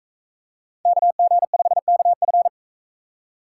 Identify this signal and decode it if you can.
Morse code, heard as a keyed tone, about 44 wpm, 700 Hz. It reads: KG5KF